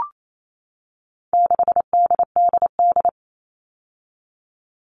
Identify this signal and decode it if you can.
Morse code, heard as a keyed tone, about 28 wpm, 700 Hz. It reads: 6DDD
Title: All Cut Numbers - 4-Digit 28wpm